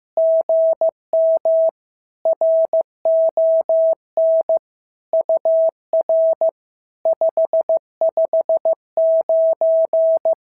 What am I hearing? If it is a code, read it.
Morse code, heard as a keyed tone, about 14 wpm, 650 Hz. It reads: GM RON UR 559
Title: SOTA QSO Practice 15wpm